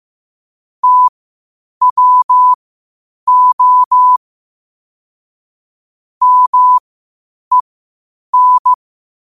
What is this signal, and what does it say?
Morse code, heard as a keyed tone, about 15 wpm, spaced as for 8 wpm, 1000 Hz. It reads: TWO MEN
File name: qotdfiles / 8wpm